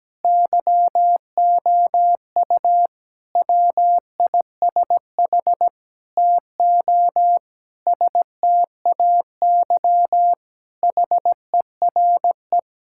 Morse code, heard as a keyed tone, about 17 wpm, 700 Hz. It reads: YOU WISH TO STAY HERE
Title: Sentences from Top 1600 Words 17wpm